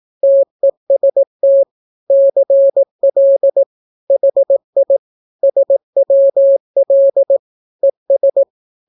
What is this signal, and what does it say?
Morse code, heard as a keyed tone, about 18 wpm, 550 Hz. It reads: TEST CL HI SWL ES
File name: naukapodstaw / skroty / skroty